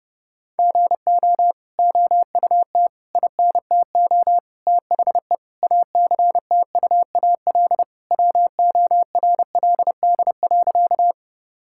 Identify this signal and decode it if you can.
Morse code, heard as a keyed tone, about 30 wpm, 700 Hz. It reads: GO OUT INTO THE ACTUAL WORLD.
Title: Sentences from Top 1000 Words - Review Part 5 - 30wpm